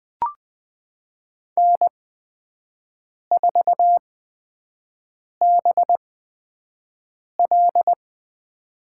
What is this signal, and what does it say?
Morse code, heard as a keyed tone, about 20 wpm, spaced as for 5 wpm, 700 Hz. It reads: N4BL
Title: USA Call Signs - 4 Characters - 8x Character Spacing - 20wpm